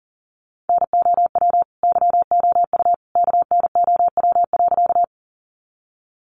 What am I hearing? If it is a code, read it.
Morse code, heard as a keyed tone, about 40 wpm, 700 Hz. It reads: NOW YOU KNOW.
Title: Sentences from Top 100 Words 40wpm